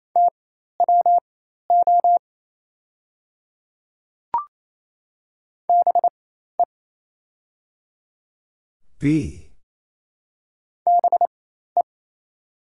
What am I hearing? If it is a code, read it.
Morse code, heard as a keyed tone, about 28 wpm, spaced as for 13 wpm, 700 Hz. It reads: TWO BE BE
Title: 4x Character Spacing - Top 100 Words - 28wpm